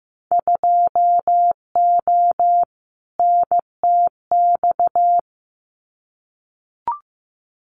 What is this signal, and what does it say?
Morse code, heard as a keyed tone, about 15 wpm, 700 Hz. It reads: 2O NTX
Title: Winter Field Day Exchanges 15wpm